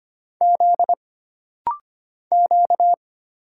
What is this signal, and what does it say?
Morse code, heard as a keyed tone, about 25 wpm, 700 Hz. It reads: Z Q
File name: Single Character (Letter:Number:Punctuation:Prosign) - Rapid-Fire - 25wpm